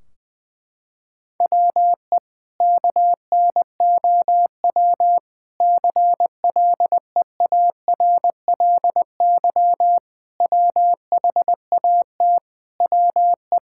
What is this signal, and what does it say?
Morse code, heard as a keyed tone, about 20 wpm, 700 Hz. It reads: WE KNOW CLEARLY WHAT WE
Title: Sentences from Top 1600 Words 20wpm